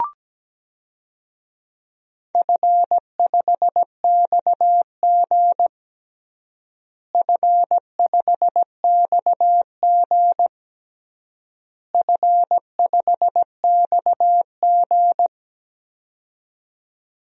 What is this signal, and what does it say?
Morse code, heard as a keyed tone, about 17 wpm, 700 Hz. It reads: F5XG F5XG F5XG
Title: DXCC-100 Call Signs - 4 Characters - Repeated 3x - 17wpm